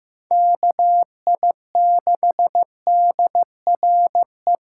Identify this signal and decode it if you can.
Morse code, heard as a keyed tone, about 15 wpm, 700 Hz. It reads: KI6DRE
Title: USA Callsign DE Callsign 15wpm